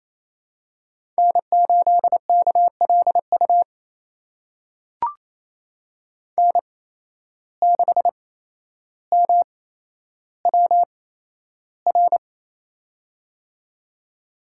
Morse code, heard as a keyed tone, about 28 wpm, spaced as for 12 wpm, 700 Hz. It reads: N8KLU N6MWR
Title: USA Call Signs - 5 Characters - 8x Character Spacing - 28wpm